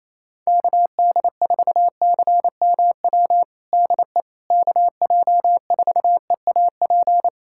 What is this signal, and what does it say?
Morse code, heard as a keyed tone, about 28 wpm, 700 Hz. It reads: KD4CMW DE KJ4EAP
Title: USA Callsign DE Callsign 28wpm